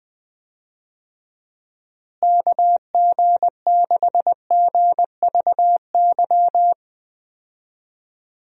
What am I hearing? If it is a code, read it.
Morse code, heard as a keyed tone, about 20 wpm, 700 Hz. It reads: KG6GVY
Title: USA Call Signs - 6 Characters - Repeated 3x - 20wpm